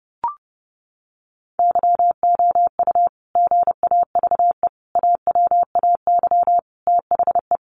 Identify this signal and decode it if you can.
Morse code, heard as a keyed tone, about 30 wpm, 700 Hz. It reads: YOU GAVE AWAY THE
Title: Sentences from Top 1300 Words 30wpm